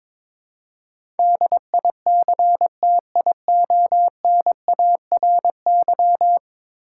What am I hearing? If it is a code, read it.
Morse code, heard as a keyed tone, about 22 wpm, 700 Hz. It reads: DICTIONARY
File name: practice / 22wpm / 10 Letter Words - First 3 Letters Given - 22wpm